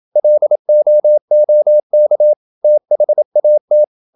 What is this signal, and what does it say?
Morse code, heard as a keyed tone, about 27 wpm, 600 Hz. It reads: LOOK THAT
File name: Never 0027wpm